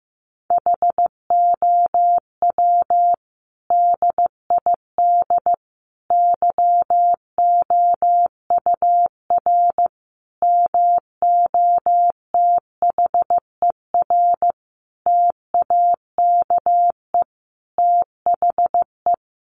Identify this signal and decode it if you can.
Morse code, heard as a keyed tone, about 15 wpm, 700 Hz. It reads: HOW DID YOUR MOTHER TAKE THE
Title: Sentences from Top 700 Words 15wpm